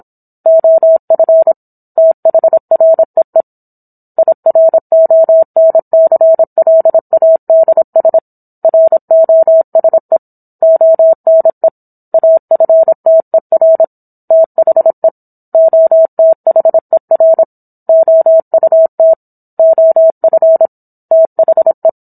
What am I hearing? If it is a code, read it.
Morse code, heard as a keyed tone, about 27 wpm, 650 Hz. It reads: OF THREE IRONCLADS ROSE ONE AFTER THE OTHER OUT OF THE